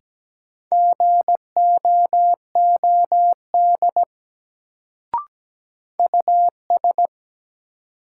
Morse code, heard as a keyed tone, about 17 wpm, 700 Hz. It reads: GOOD US
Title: Sets of 2 Words - New Word Drill - 17wpm